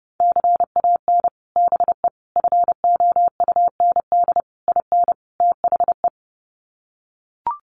Morse code, heard as a keyed tone, about 30 wpm, 700 Hz. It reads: CAN BE FOUND IN THE
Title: Sets of 5 Words Encore 30wpm